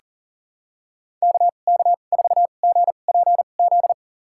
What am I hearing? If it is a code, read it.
Morse code, heard as a keyed tone, about 39 wpm, spaced as for 29 wpm, 700 Hz. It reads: KK4GPZ
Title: USA Call Signs - 6 Characters - 2x Character Spacing - 40wpm